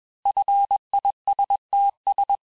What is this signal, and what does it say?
Morse code, heard as a keyed tone, about 21 wpm, 800 Hz. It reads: FISTS
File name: teste-FIN-de-QSO-vitesse-20mpm